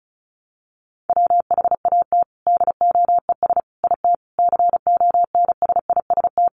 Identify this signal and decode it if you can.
Morse code, heard as a keyed tone, about 34 wpm, 700 Hz. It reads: WHAT DOES IT CONSIST
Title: Sentences from Top 1400 Words 35wpm